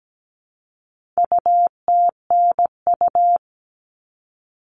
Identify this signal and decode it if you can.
Morse code, heard as a keyed tone, about 17 wpm, 700 Hz. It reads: UTNU
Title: All Cut Numbers - 4-Digit 17wpm